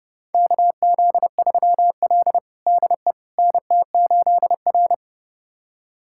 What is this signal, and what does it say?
Morse code, heard as a keyed tone, about 29 wpm, 700 Hz. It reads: KZ3L DE NT8R
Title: USA Callsign DE Callsign - 4 Characters - 30wpm